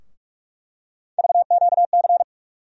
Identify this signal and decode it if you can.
Morse code, heard as a keyed tone, about 42 wpm, 700 Hz. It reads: VQC
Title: USA Call Sign Postfixes - 3 Letters 45wpm